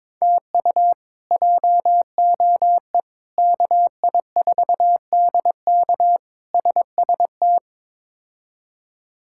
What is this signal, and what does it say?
Morse code, heard as a keyed tone, about 21 wpm, 700 Hz. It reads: TU JOE KI4DK SST
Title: K1USN SST QSO Practice - 22wpm